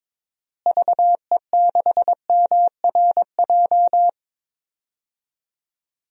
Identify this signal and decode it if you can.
Morse code, heard as a keyed tone, about 23 wpm, 700 Hz. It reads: VE6MRJ
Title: DXCC-100 Call Signs - 6 Characters - Repeated 2x - 22wpm